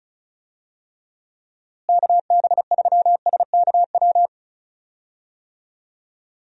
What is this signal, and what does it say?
Morse code, heard as a keyed tone, about 36 wpm, 700 Hz. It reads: KB3SKW
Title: USA Call Signs - 6 Characters - Repeated 3x - 35wpm